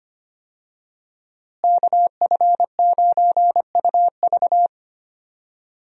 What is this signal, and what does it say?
Morse code, heard as a keyed tone, about 25 wpm, 700 Hz. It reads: KF9UV
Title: USA Call Signs - 5 Characters - Repeated 3x - 25wpm